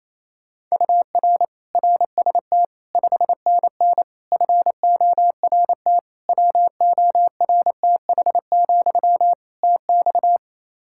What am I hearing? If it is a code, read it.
Morse code, heard as a keyed tone, about 28 wpm, 700 Hz. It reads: UR RST 5NN FORT WORTH, TX
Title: Practice QSO - Exchange 1 - Beginning - 28wpm